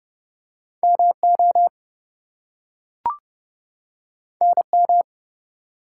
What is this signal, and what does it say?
Morse code, heard as a keyed tone, about 30 wpm, 700 Hz. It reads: MO NM
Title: US States and Canadian Provinces 30wpm